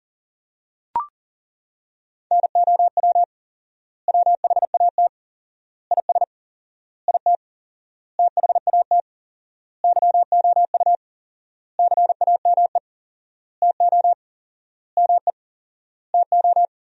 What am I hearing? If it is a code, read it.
Morse code, heard as a keyed tone, about 40 wpm, 700 Hz. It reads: NOW WHAT IS IT THAT YOU CAME TO ME TO
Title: 4x Word-Spacing - Sentences from Top 200 Words - 40wpm